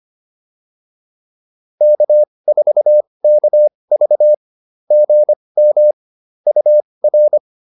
Morse code, heard as a keyed tone, about 25 wpm, spaced as for 20 wpm, 590 Hz. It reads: K4KV GM UR